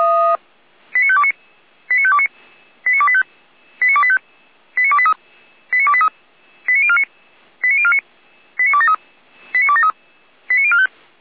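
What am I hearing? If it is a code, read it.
Five-tone selective call (ZVEI), heard as quick runs of five tones, tones of about 70 ms. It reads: 89529 89529 89285 89285 89283 89283 89049 89049 89273 89273 89045
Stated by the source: ZVEI five-tone call received on SDR, from the Feuerwehr Orlamünde fire brigade audio server.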